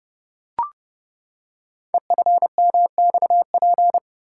Morse code, heard as a keyed tone, about 28 wpm, 700 Hz. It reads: EFMXP
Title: Code Groups - 5 Letters - 30wpm